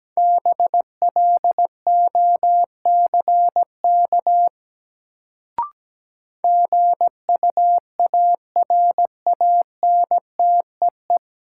Morse code, heard as a keyed tone, about 17 wpm, 700 Hz. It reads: BLOCK GUARANTEE
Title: Sentences from Top 1600 Words - New Word Drill Alt - 17wpm